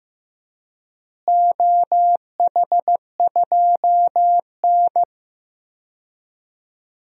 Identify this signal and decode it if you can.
Morse code, heard as a keyed tone, about 15 wpm, 700 Hz. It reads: OH2N
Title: DXCC-100 Call Signs - 4 Characters - Repeated 2x - 15wpm